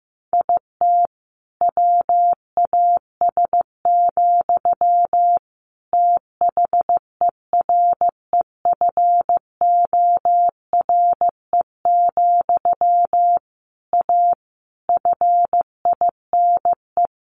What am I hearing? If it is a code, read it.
Morse code, heard as a keyed tone, about 15 wpm, 700 Hz. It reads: IT WAS, THEREFORE, A FINE
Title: Sentences from Top 1400 Words 15wpm